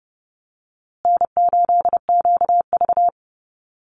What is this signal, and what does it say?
Morse code, heard as a keyed tone, about 30 wpm, 700 Hz. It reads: N8QV
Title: USA Call Signs Repeated 2x 30wpm